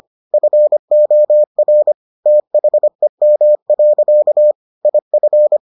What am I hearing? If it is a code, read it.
Morse code, heard as a keyed tone, about 25 wpm, 600 Hz. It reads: FOR THEM. IF